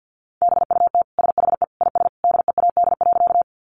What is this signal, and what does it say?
Morse code, heard as a keyed tone, about 50 wpm, 700 Hz. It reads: BUT SHE IS DEAD.
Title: Sentences from Top 1500 Words 50wpm